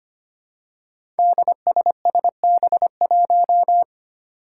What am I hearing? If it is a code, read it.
Morse code, heard as a keyed tone, about 25 wpm, 700 Hz. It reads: DSSB1
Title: Five Challenging Letter-Number Combinations 25wpm